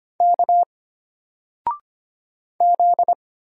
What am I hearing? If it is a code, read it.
Morse code, heard as a keyed tone, about 25 wpm, 700 Hz. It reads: K Z
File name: Single-Letter-Number-25wpm